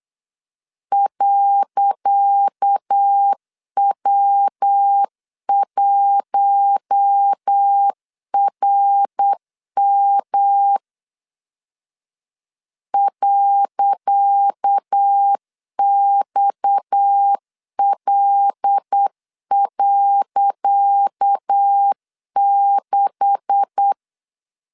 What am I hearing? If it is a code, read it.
Morse code, heard as a keyed tone, about 8 wpm, 800 Hz. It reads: .W1RM .XL.6
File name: LEZIONI / 10 SIGNS / 10C DOT 8wpm 3649